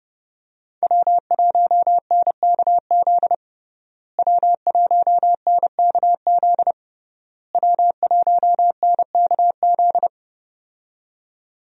Morse code, heard as a keyed tone, about 30 wpm, 700 Hz. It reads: W1NKZ W1NKZ W1NKZ
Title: USA Call Signs - 5 Characters - Repeated 3x - 30wpm